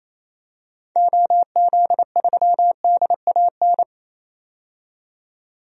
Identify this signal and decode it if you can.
Morse code, heard as a keyed tone, about 28 wpm, 700 Hz. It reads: OZ3DAN
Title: DXCC-100 Call Signs - 6 Characters - 28wpm